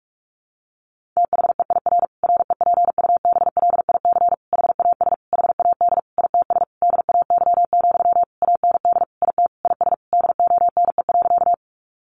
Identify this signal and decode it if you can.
Morse code, heard as a keyed tone, about 44 wpm, 700 Hz. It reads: THEIR REPUBLIC HAS HAD ITS DAY, AND IT IS DONE.
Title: Sentences from Top 1500 Words 45wpm